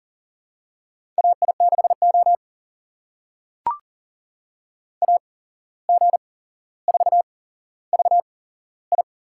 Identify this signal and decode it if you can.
Morse code, heard as a keyed tone, about 43 wpm, spaced as for 16 wpm, 700 Hz. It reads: AI6O AG4VI
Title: USA Call Signs - 8x Character Spacing - 40wpm